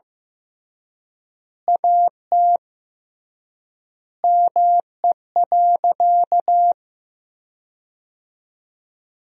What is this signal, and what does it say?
Morse code, heard as a keyed tone, about 15 wpm, 700 Hz. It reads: AT ME.
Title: 3x Word-Spacing - Sentences from Top 200 Words - 15wpm